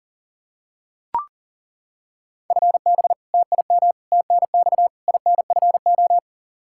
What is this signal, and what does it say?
Morse code, heard as a keyed tone, about 37 wpm, 700 Hz. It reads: FB TIM TNX INFO
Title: Practice QSO - Exchange 3 - More Info - 40wpm